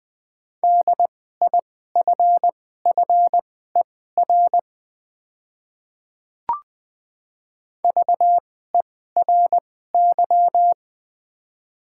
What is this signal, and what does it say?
Morse code, heard as a keyed tone, about 20 wpm, spaced as for 14 wpm, 700 Hz. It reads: DIFFER VERY
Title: 2x Character Spacing - Top 200 Words - 20wpm